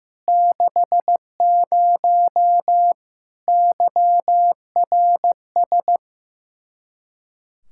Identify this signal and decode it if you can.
Morse code, heard as a keyed tone, about 15 wpm, 700 Hz. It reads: 60 YRS
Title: Practice QSO - Exchange 3 - More Info - 15wpm